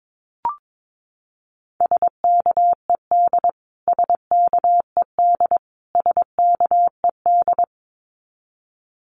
Morse code, heard as a keyed tone, about 22 wpm, 700 Hz. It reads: SKED SKED SKED
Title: Abbreviations Repeated 3x - 22wpm